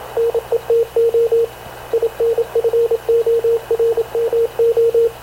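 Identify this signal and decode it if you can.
Morse code, heard as a keyed tone, about 27 wpm, 460 Hz, with radio noise. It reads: NETO INFORMO
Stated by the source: Havana Radio CLA press (PX) broadcast, recorded by Radio Officer Paul Dumesnil aboard mv CNAV QUEST/CZDO.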